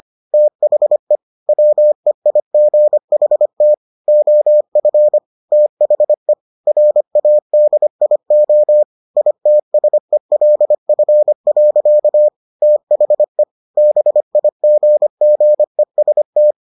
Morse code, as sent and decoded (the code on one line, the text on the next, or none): - .... / .-- . .. --. .... - / --- ..-. / - .... / .-. .- -.. .. --- / .. - ... .-.. ..-. .-.-.- / - .... / -... .. --. --. ... -
THE WEIGHT OF THE RADIO ITSELF. THE BIGGEST